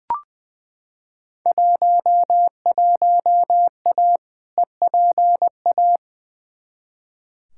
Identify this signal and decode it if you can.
Morse code, heard as a keyed tone, about 20 wpm, 700 Hz. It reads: 11A EPA
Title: Field Day Exchanges Encore - 20wpm